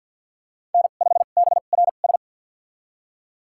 Morse code, heard as a keyed tone, about 51 wpm, spaced as for 36 wpm, 700 Hz. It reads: N5BRS
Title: USA Call Signs - 5 Characters - 2x Character Spacing - 50wpm